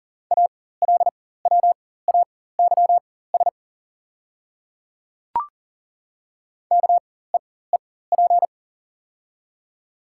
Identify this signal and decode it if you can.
Morse code, heard as a keyed tone, about 40 wpm, spaced as for 19 wpm, 700 Hz. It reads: ALWAYS KEEP
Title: 4x Character Spacing - Top 300 Words - 40wpm